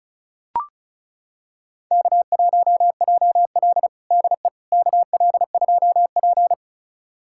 Morse code, heard as a keyed tone, about 34 wpm, 700 Hz. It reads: K1JL DE KL2P
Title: USA Callsign DE Callsign - 4 Characters - 35wpm